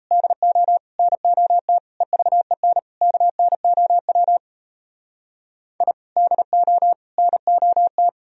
Morse code, heard as a keyed tone, about 37 wpm, 700 Hz. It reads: DO NOT EVEN KNOW I DO NOT